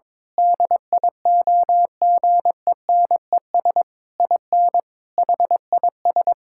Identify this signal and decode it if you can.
Morse code, heard as a keyed tone, about 22 wpm, 700 Hz. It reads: DIOGENES IN HIS
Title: Pangram Challenge v2 - 22wpm